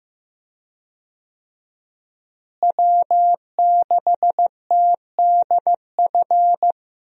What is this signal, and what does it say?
Morse code, heard as a keyed tone, about 15 wpm, 700 Hz. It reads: W6TDF
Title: USA Call Signs - 5 Characters - Repeated 3x - 15wpm